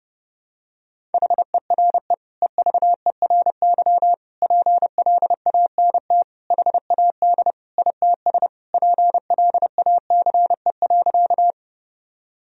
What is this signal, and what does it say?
Morse code, heard as a keyed tone, about 30 wpm, 700 Hz. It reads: HERE EVERY PLANT HAD ITS PLACE.
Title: Sentences from Top 1000 Words - Review Part 1 - 30wpm